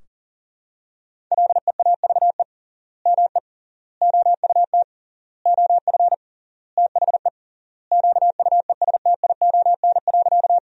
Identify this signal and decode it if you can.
Morse code, heard as a keyed tone, about 37 wpm, 700 Hz. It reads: LEAVE ME OUT OF THE QUESTION.
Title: 3x Word-Spacing - Sentences from Top 400 Words - 40wpm